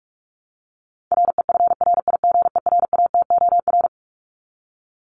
Morse code, heard as a keyed tone, about 44 wpm, 700 Hz. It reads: REFRIGERATOR
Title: Top 1501-1750 Words 45wpm